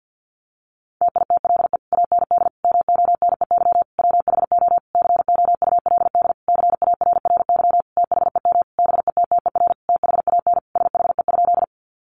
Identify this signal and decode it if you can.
Morse code, heard as a keyed tone, about 48 wpm, 700 Hz. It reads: TITLE AND MONEY WHO COULD CARRY THEM BETTER THAN SHE?
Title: Sentences from Top 1200 Words 50wpm